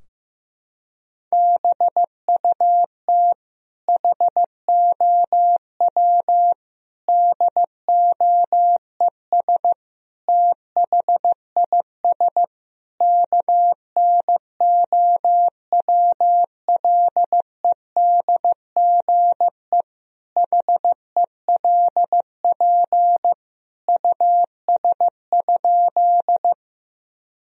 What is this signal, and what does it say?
Morse code, heard as a keyed tone, about 16 wpm, 700 Hz. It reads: BUT HOW DOES THIS KNOWLEDGE HELP US?
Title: Sentences from Top 1000 Words - Review Part 3 - 15wpm